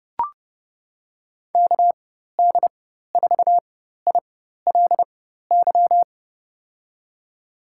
Morse code, heard as a keyed tone, about 29 wpm, spaced as for 14 wpm, 700 Hz. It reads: KD4ILY
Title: USA Call Signs - 6 Characters - 4x Character Spacing - 30wpm